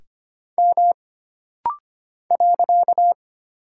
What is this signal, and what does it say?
Morse code, heard as a keyed tone, about 25 wpm, 700 Hz. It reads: M .
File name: Single Character (Letter:Number:Punctuation:Prosign) - Rapid-Fire - 25wpm